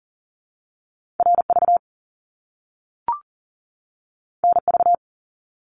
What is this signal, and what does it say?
Morse code, heard as a keyed tone, about 40 wpm, 700 Hz. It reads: RV NV
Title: Top Two Letter Combinations in Words 40wpm